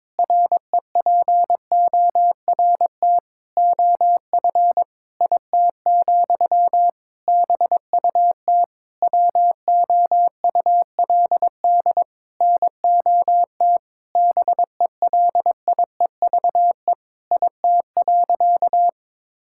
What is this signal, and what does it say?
Morse code, heard as a keyed tone, about 22 wpm, 700 Hz. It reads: REPORT OF IT, BUT WOULD NOT BELIEVE IT.
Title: Sentences from Top 1000 Words - Review Part 6 - 22wpm